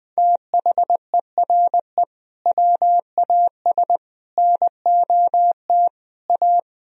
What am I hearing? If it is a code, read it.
Morse code, heard as a keyed tone, about 21 wpm, 700 Hz. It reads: THERE WAS NOT A
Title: Sentences from Top 1000 Words - Review Part 4 - 20wpm